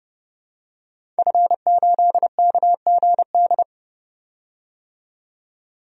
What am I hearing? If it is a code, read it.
Morse code, heard as a keyed tone, about 30 wpm, 700 Hz. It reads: F8KGD
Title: DXCC-100 Call Signs - 5 Characters - 30wpm